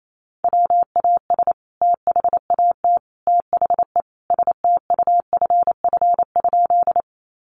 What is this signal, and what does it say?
Morse code, heard as a keyed tone, about 28 wpm, 700 Hz. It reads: WAS THAT THE STUFF?
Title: Sentences from Top 1400 Words 28wpm